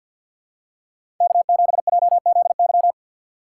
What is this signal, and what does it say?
Morse code, heard as a keyed tone, about 50 wpm, 700 Hz. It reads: K7JZY
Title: USA Call Signs - 4x Character Spacing - 50wpm